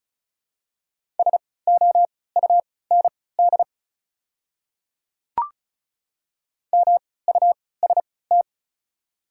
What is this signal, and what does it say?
Morse code, heard as a keyed tone, about 36 wpm, spaced as for 20 wpm, 700 Hz. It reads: SOUND MUST
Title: 3x Character Spacing - Top 200 Words - 35wpm